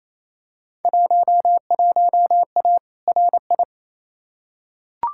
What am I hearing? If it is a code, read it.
Morse code, heard as a keyed tone, about 28 wpm, 700 Hz. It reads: 11A RI
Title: Field Day Exchanges Encore - 28wpm